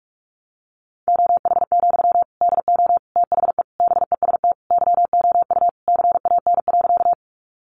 Morse code, heard as a keyed tone, about 44 wpm, 700 Hz. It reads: OH, DO THE BEST YOU CAN.
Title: Sentences from Top 500 Words 45wpm